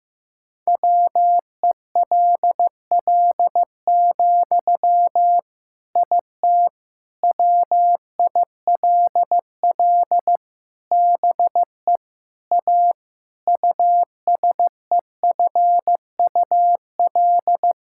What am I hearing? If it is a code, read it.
Morse code, heard as a keyed tone, about 15 wpm, 700 Hz. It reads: WELL, IT WILL BE A USEFUL
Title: Sentences from Top 1000 Words - Review Part 6 - 15wpm